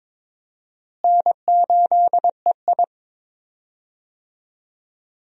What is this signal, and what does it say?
Morse code, heard as a keyed tone, about 22 wpm, 700 Hz. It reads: N8EI
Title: USA Call Signs - 4 Characters - Repeated 2x - 22wpm